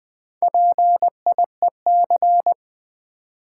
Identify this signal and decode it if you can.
Morse code, heard as a keyed tone, about 20 wpm, 700 Hz. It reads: PIEC
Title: Build A Word - 1 Letter At A Time - 3 to 5-Letter Words - 20wpm